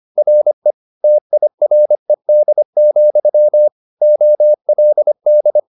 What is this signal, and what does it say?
Morse code, heard as a keyed tone, about 25 wpm, 600 Hz. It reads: RE TIRED, OLD